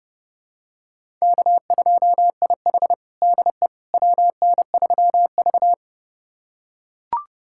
Morse code, heard as a keyed tone, about 32 wpm, 700 Hz. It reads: K2IH DE WN3V
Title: USA Callsign DE Callsign - 4 Characters - 30wpm